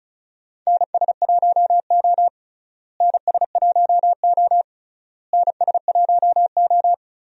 Morse code, heard as a keyed tone, about 34 wpm, 700 Hz. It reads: NS1O NS1O NS1O
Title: USA Call Signs - 4 Characters - Repeated 3x - 35wpm